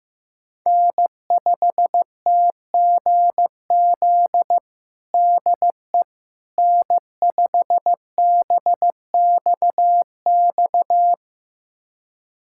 Morse code, heard as a keyed tone, about 16 wpm, 700 Hz. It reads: N5TGZ DE N5BXX
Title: USA Callsign DE Callsign - 5 Characters - 15wpm